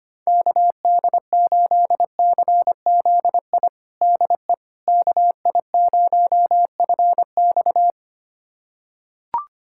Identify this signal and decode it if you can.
Morse code, heard as a keyed tone, about 25 wpm, 700 Hz. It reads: KD8CZI DE KI0FX
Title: USA Callsign DE Callsign 25wpm